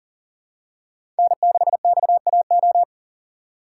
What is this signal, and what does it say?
Morse code, heard as a keyed tone, about 40 wpm, 700 Hz. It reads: N6XAO
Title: USA Call Signs - 5 Characters - 4x Character Spacing - 40wpm